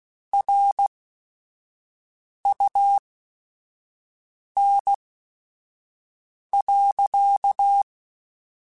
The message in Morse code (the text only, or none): RUN.